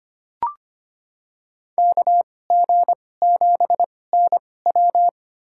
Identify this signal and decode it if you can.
Morse code, heard as a keyed tone, about 25 wpm, spaced as for 18 wpm, 700 Hz. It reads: KG7NW
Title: USA Call Signs - 5 Characters - 2x Character Spacing - 25wpm